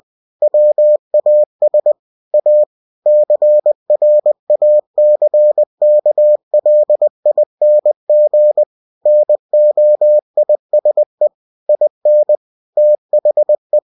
Morse code, heard as a keyed tone, about 20 wpm, 600 Hz. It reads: WAS A CRACKLING NOISE IN THE
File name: cw-books / winnie-the-pooh / winnie-20wpm-page-0044